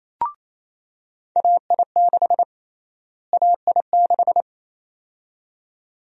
Morse code, heard as a keyed tone, about 28 wpm, 700 Hz. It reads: AI6 AI6
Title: USA Call Sign Prefixes - Repeated 2x - 28wpm